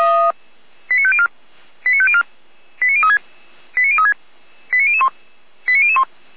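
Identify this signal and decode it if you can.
Five-tone selective call (ZVEI), heard as quick runs of five tones, tones of about 70 ms. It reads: 89504 89504 89036 89036 89001 89001
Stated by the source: ZVEI five-tone selective call received by SDR, from feuerwehr-orlamuende.org fire brigade server.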